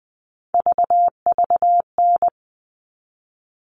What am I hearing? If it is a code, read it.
Morse code, heard as a keyed tone, about 20 wpm, 700 Hz. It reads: VVN
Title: All Cut Numbers - 3-Digit 20wpm